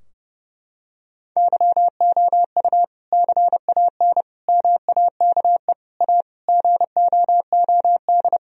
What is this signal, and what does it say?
Morse code, heard as a keyed tone, about 32 wpm, 700 Hz. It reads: YOU CAN MAKE A GOOD
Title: Sentences from Top 1000 Words - Review Part 4 - 30wpm